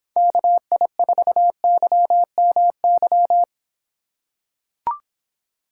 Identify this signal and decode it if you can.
Morse code, heard as a keyed tone, about 26 wpm, 700 Hz. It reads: KI4YMY